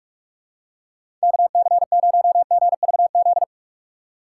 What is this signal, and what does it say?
Morse code, heard as a keyed tone, about 43 wpm, 700 Hz. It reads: KC0GVZ
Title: USA Call Signs - 6 Characters - 4x Character Spacing - 45wpm